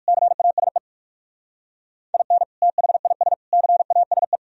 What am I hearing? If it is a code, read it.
Morse code, heard as a keyed tone, about 62 wpm, spaced as for 54 wpm, 700 Hz. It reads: CASE IN THIS CASE
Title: Speed-Racing - Sets of 3 Words 45wpm